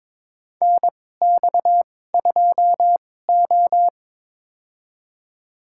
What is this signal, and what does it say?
Morse code, heard as a keyed tone, about 22 wpm, spaced as for 16 wpm, 700 Hz. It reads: NX2O
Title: USA Call Signs - 2x Character Spacing - 22wpm